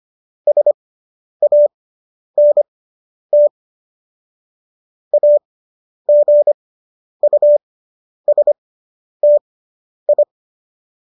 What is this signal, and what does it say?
Morse code, heard as a keyed tone, about 25 wpm, spaced as for 10 wpm, 590 Hz. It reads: SANT AGUSTI